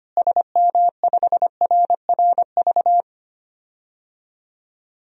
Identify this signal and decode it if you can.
Morse code, heard as a keyed tone, about 25 wpm, 700 Hz. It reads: SM5RRV
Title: DXCC-100 Call Signs - 6 Characters - Repeated 3x - 25wpm